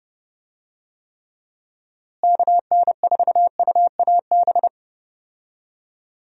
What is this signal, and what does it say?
Morse code, heard as a keyed tone, about 30 wpm, 700 Hz. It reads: KN4UAB
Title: USA Call Signs - 6 Characters - Repeated 2x - 30wpm